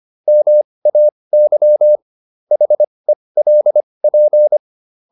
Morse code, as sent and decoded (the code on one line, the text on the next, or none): -- .- -.-- / .... .-.. .--.
MAY HELP